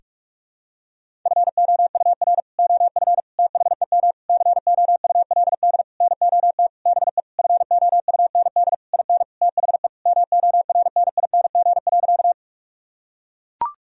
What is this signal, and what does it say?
Morse code, heard as a keyed tone, about 42 wpm, 700 Hz. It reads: FOUR OF THEM COULD NOT BE FOUND IN THE MORNING.
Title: Sentences from Top 1000 Words - Review Part 1 - 45wpm